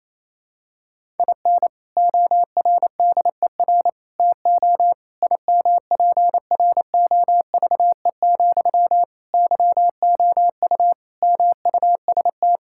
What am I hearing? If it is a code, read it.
Morse code, heard as a keyed tone, about 28 wpm, 700 Hz. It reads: IN ORDER TO IMPROVE, YOU MUST
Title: Sentences from Top 1000 Words - Review Part 5 - 28wpm